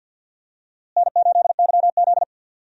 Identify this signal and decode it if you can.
Morse code, heard as a keyed tone, about 50 wpm, 700 Hz. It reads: N8YZ
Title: USA Call Signs - 4 Characters - Repeated 3x - 50wpm